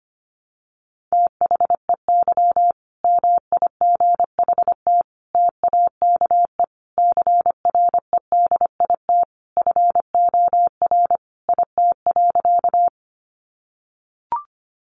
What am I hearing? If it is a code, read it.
Morse code, heard as a keyed tone, about 25 wpm, 700 Hz. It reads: THEY MIGHT TAKE CREDIT FOR IT.
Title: Sentences from Top 1000 Words 25wpm